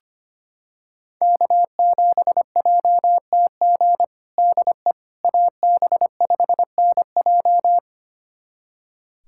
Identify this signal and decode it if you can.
Morse code, heard as a keyed tone, about 25 wpm, 700 Hz. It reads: K7JTG DE AB5NJ
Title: USA Callsign DE Callsign - 5 Characters - 25wpm